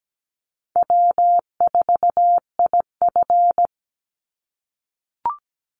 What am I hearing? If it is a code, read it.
Morse code, heard as a keyed tone, about 17 wpm, 700 Hz. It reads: W4IF
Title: Easy Call Signs - 17wpm